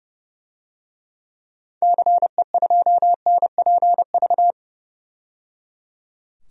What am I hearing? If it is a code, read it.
Morse code, heard as a keyed tone, about 30 wpm, 700 Hz. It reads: CE2NPV
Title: DXCC-100 Call Signs - 6 Characters - 30wpm